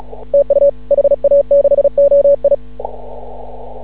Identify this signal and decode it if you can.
Morse code, heard as a keyed tone, about 36 wpm, 560 Hz. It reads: TU HA6OI
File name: CW / ha6oi 2